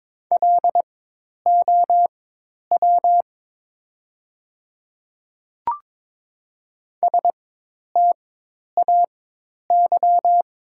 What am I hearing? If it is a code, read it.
Morse code, heard as a keyed tone, about 22 wpm, spaced as for 10 wpm, 700 Hz. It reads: LOW STAY
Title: 4x Character Spacing - Top 500 Words - 22wpm